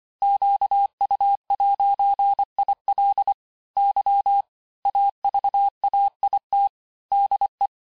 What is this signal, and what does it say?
Morse code, heard as a keyed tone, about 25 wpm, 800 Hz. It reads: QU'IL Y AVAIT DE
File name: cours-868-teste-clair-vitesse-23mpm